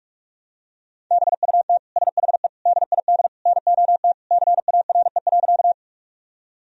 Morse code, heard as a keyed tone, about 46 wpm, 700 Hz. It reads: BUT SHE DID NOT CARE.
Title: Sentences from Top 1000 Words - Review Part 1 - 45wpm